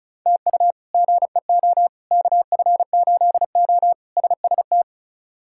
Morse code, heard as a keyed tone, about 36 wpm, 700 Hz. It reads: TU GEO KF8O SST
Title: K1USN SST QSO Practice - 35wpm